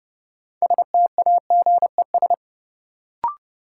Morse code, heard as a keyed tone, about 30 wpm, 700 Hz. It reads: STAGES
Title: Sets of 5 Words Encore - New Word Drill - 30wpm